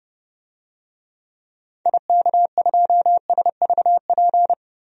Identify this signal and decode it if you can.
Morse code, heard as a keyed tone, about 30 wpm, 700 Hz. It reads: IK2SVP
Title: DXCC-100 Call Signs - 6 Characters - 30wpm